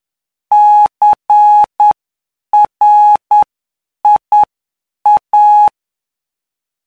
Morse code, heard as a keyed tone, about 9 wpm, spaced as for 7 wpm, 840 Hz. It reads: CRIA
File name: cw / py2lcd / 17-PY2LCD-Li��o 26